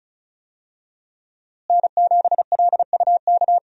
Morse code, heard as a keyed tone, about 36 wpm, 700 Hz. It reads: N7LUK
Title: USA Call Signs - 5 Characters - Repeated 3x - 35wpm